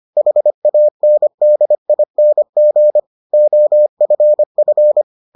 Morse code, heard as a keyed tone, about 25 wpm, 600 Hz. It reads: HANDING OFF